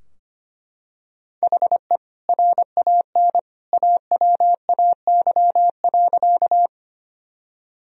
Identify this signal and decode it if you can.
Morse code, heard as a keyed tone, about 25 wpm, 700 Hz. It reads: HE RAN AWAY.
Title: Sentences from Top 1000 Words - Review Part 3 - 25wpm